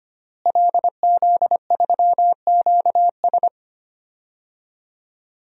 Morse code, heard as a keyed tone, about 25 wpm, 700 Hz. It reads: LZ3QS
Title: DXCC-100 Call Signs - 5 Characters - Repeated 3x - 25wpm